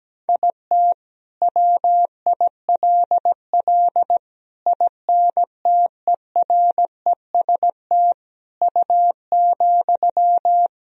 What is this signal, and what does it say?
Morse code, heard as a keyed tone, about 17 wpm, 700 Hz. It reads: IT WILL INTEREST U,
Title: Sentences from Top 400 Words and Abbreviations 17wpm